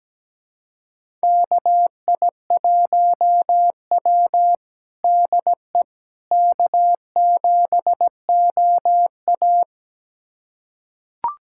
Morse code, heard as a keyed tone, about 17 wpm, 700 Hz. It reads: KI1W DE K7OA
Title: USA Callsign DE Callsign - 4 Characters - 17wpm